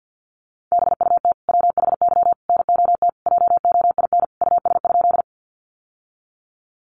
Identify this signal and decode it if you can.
Morse code, heard as a keyed tone, about 50 wpm, 700 Hz. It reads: BUT WHY NOT JOIN US?
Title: Sentences from Top 1400 Words 50wpm